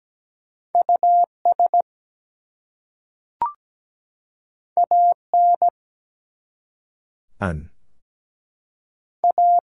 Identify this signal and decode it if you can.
Morse code, heard as a keyed tone, about 17 wpm, 700 Hz. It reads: US AN A
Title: Two Letter Words - 17wpm